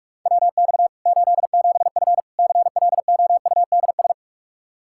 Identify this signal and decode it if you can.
Morse code, heard as a keyed tone, about 46 wpm, 700 Hz. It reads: WX 87F CLOUDS
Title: K1USN SST QSO Practice 45wpm